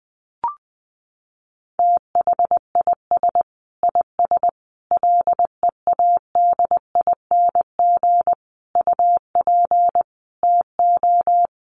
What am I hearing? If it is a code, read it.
Morse code, heard as a keyed tone, about 20 wpm, 700 Hz. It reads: THIS IS LEADING UP TO
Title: Sentences from Top 1100 Words 20wpm